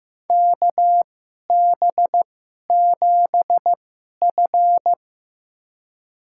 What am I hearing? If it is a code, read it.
Morse code, heard as a keyed tone, about 15 wpm, spaced as for 11 wpm, 700 Hz. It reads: KB7F
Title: USA Call Signs - 2x Character Spacing - 15wpm